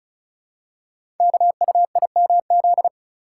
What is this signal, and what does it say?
Morse code, heard as a keyed tone, about 36 wpm, 700 Hz. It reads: KUIMZ
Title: Code Groups - 5 Letters - 35wpm